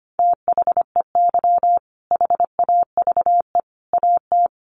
Morse code, heard as a keyed tone, about 25 wpm, 700 Hz. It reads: THEY HAVE AT
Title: Sentences from Top 900 Words 25wpm